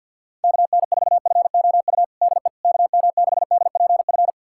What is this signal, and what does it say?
Morse code, heard as a keyed tone, about 48 wpm, 700 Hz. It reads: KN4FOU DE KM6DPF
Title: USA Callsign DE Callsign - 6 Characters - 50wpm